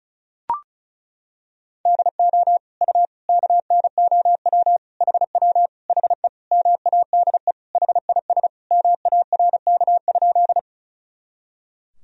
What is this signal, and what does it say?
Morse code, heard as a keyed tone, about 36 wpm, 700 Hz. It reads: DO U KNOW HW HE MADE HIS MARK?
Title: Sentences from Top 300 Words and Abbreviations 35wpm